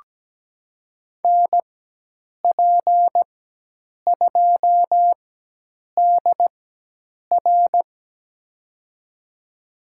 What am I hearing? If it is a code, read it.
Morse code, heard as a keyed tone, about 17 wpm, spaced as for 8 wpm, 700 Hz. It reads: NP2DR
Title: USA Call Signs - 4x Character Spacing - 17wpm